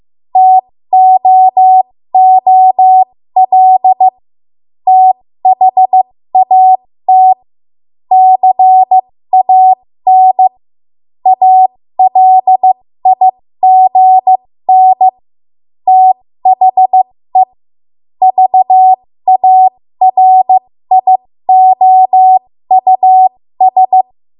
Morse code, heard as a keyed tone, about 15 wpm, spaced as for 13 wpm, 750 Hz. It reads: TOOL THAT CAN ALIGN THE VARIOUS
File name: Morse / 13WPM